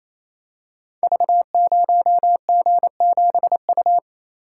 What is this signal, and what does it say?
Morse code, heard as a keyed tone, about 27 wpm, 700 Hz. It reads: V0G7U
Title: Code Groups - 5 Letter:Numbers - 28wpm